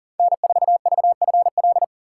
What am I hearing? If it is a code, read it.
Morse code, heard as a keyed tone, about 40 wpm, 700 Hz. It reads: N4VFL
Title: USA Call Signs - 5 Characters - 2x Character Spacing - 40wpm